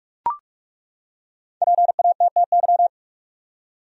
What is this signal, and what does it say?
Morse code, heard as a keyed tone, about 42 wpm, 700 Hz. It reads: PATTY
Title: USA Common Names - Max 5 Letters - 45wpm